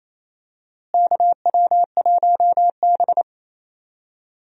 Morse code, heard as a keyed tone, about 28 wpm, 700 Hz. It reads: KW1B
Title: USA Call Signs - 4 Characters - Repeated 3x - 28wpm